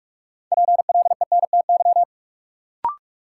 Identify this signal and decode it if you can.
Morse code, heard as a keyed tone, about 47 wpm, 700 Hz. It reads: PLENTY
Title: Sentences from Top 1300 Words - New Word Drill Alt - 45wpm